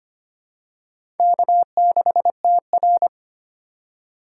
Morse code, heard as a keyed tone, about 25 wpm, 700 Hz. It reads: K6TR
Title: USA Call Signs - 4 Characters - 3x Character Spacing - 25wpm